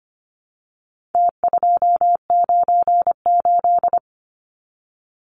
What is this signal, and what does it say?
Morse code, heard as a keyed tone, about 25 wpm, 700 Hz. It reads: T298
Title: Numbers - Contest Style - 4-Digit 25wpm